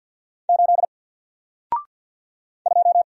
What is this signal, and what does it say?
Morse code, heard as a keyed tone, about 48 wpm, 700 Hz. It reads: 8 2
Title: Single Letter-Number Alt 50wpm